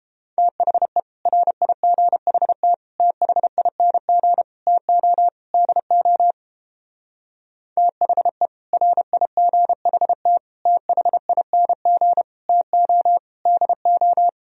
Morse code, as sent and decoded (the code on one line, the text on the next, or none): - .... / .-. .. --. .... - / - .... .. -. --. / - --- / -.. --- / - .... / .-. .. --. .... - / - .... .. -. --. / - --- / -.. ---
THE RIGHT THING TO DO THE RIGHT THING TO DO